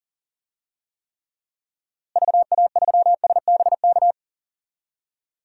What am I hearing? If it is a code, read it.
Morse code, heard as a keyed tone, about 37 wpm, 700 Hz. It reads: VA3SBK
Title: DXCC-100 Call Signs - 6 Characters - Repeated 2x - 40wpm